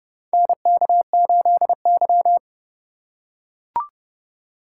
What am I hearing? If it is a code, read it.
Morse code, heard as a keyed tone, about 31 wpm, 700 Hz. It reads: NK8Y
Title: USA Call Signs - 4 Characters - 4x Character Spacing - 30wpm